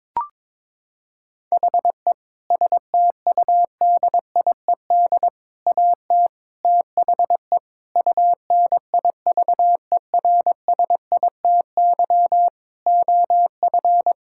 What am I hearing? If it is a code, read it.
Morse code, heard as a keyed tone, about 21 wpm, 700 Hz. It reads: HE STUDIED AT THE UNIVERSITY OF
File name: practice / 22wpm / Sentences with 4-Word Phrases - 22wpm